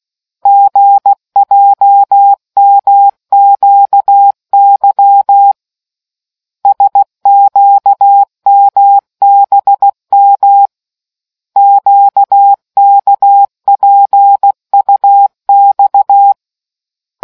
Morse code, heard as a keyed tone, about 16 wpm, 800 Hz. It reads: GJMQY SQMBM QKPUX